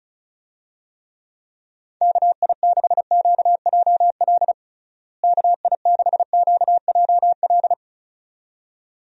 Morse code, heard as a keyed tone, about 36 wpm, 700 Hz. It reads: KI6QJL KI6QJL
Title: USA Call Signs - 6 Characters - Repeated 2x - 35wpm